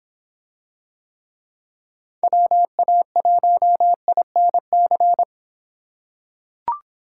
Morse code, heard as a keyed tone, about 26 wpm, 700 Hz. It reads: WA1INC